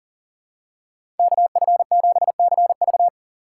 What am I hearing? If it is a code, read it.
Morse code, heard as a keyed tone, about 40 wpm, 700 Hz. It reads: KF7CV
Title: USA Call Signs - 5 Characters - 3x Character Spacing - 40wpm